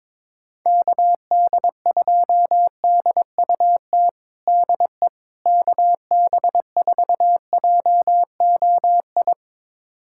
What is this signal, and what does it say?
Morse code, heard as a keyed tone, about 22 wpm, 700 Hz. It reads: KD2DUT DE KB4JOI
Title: USA Callsign DE Callsign - 6 Characters - 22wpm